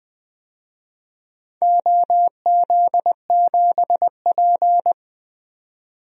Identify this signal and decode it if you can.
Morse code, heard as a keyed tone, about 20 wpm, 700 Hz. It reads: OZ7P
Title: DXCC-100 Call Signs - 4 Characters - 20wpm